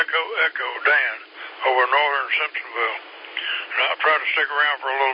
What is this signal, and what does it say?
Emergency Alert System (SAME) tones, heard as harsh buzzing bursts, with speech. It is heard from 0.8 to 4.7 s.